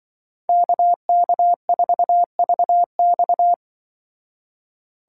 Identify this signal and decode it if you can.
Morse code, heard as a keyed tone, about 24 wpm, 700 Hz. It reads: KK4VX
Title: Speed Racing - USA 2x2 Call Signs 20wpm